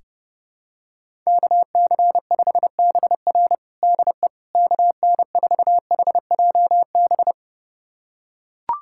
Morse code, heard as a keyed tone, about 30 wpm, 700 Hz. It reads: KC5BR DE KN4HJB
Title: USA Callsign DE Callsign 30wpm